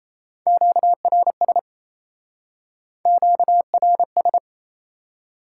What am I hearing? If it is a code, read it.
Morse code, heard as a keyed tone, about 37 wpm, 700 Hz. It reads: QRS QRS